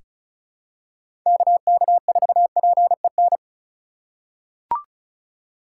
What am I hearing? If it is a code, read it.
Morse code, heard as a keyed tone, about 36 wpm, 700 Hz. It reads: KK4PEN
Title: USA Call Signs - 3x Character Spacing - 35wpm